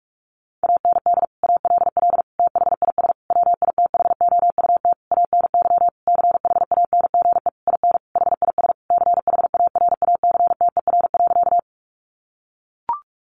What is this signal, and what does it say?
Morse code, heard as a keyed tone, about 44 wpm, 700 Hz. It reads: AND ALL THIS WITHOUT ANY CHANGE IN HIS CHARACTER.
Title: Sentences from Top 1200 Words 45wpm